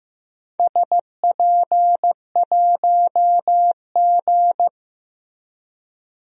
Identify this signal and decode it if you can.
Morse code, heard as a keyed tone, about 15 wpm, 700 Hz. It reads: SP1G
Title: DXCC-100 Call Signs - 4 Characters - Repeated 2x - 15wpm